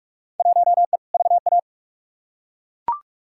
Morse code, heard as a keyed tone, about 42 wpm, 700 Hz. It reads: 1E VA
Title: Field Day Exchanges - 45wpm